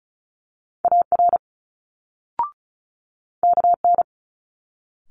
Morse code, heard as a keyed tone, about 36 wpm, 700 Hz. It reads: AR KN
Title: Multi-letter Phonemes Quick Fire 35wpm